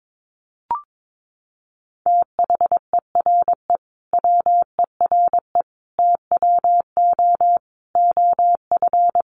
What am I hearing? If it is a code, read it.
Morse code, heard as a keyed tone, about 22 wpm, 700 Hz. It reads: THERE WERE TWO OF
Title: Sentences from Top 100 Words 22wpm